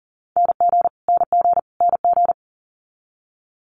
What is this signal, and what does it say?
Morse code, heard as a keyed tone, about 38 wpm, 700 Hz. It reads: NG NG NG
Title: Multi-letter Phonemes Repeated 3x 40wpm